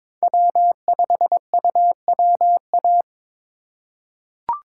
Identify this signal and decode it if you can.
Morse code, heard as a keyed tone, about 22 wpm, 700 Hz. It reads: W5UWA
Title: USA Call Signs - 5 Characters - 4x Character Spacing - 22wpm